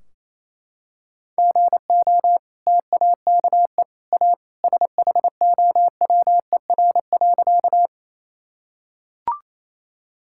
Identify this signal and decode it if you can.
Morse code, heard as a keyed tone, about 28 wpm, 700 Hz. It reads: GO TAKE A SHOWER.
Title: Sentences from Top 1600 Words 28wpm